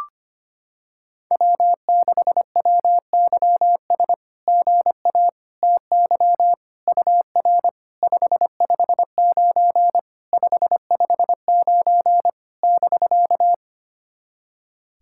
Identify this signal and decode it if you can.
Morse code, heard as a keyed tone, about 25 wpm, 700 Hz. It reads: W6WYS GA TY UR 559 559 <BK>
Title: SOTA QSO Practice 25wpm